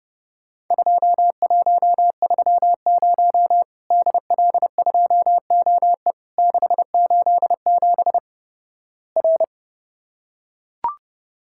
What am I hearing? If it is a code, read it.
Morse code, heard as a keyed tone, about 32 wpm, 700 Hz. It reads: 2130 DL2OE 687 R
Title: Worked All Europe DX Contest - QTC Exchange - 30wpm